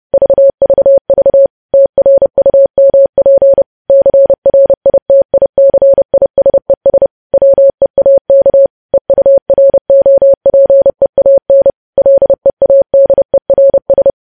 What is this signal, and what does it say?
Morse code, heard as a keyed tone, about 30 wpm, 550 Hz. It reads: VVV TRUMP CRITICISES WEAK EUROPEAN LEADERS